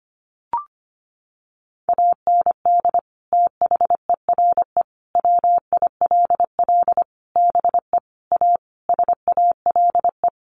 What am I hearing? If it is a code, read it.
Morse code, heard as a keyed tone, about 25 wpm, 700 Hz. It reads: AND THERE WILL BE A SALE
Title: Sentences from Top 1400 Words 25wpm